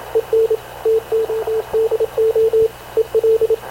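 Morse code, heard as a keyed tone, about 26 wpm, 440 Hz, with radio noise. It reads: EN TODO EL
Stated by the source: Havana Radio / CLA sending Spanish press (PX) in Morse, recorded by Radio Officer Paul Dumesnil onboard CNAV QUEST/CZDO.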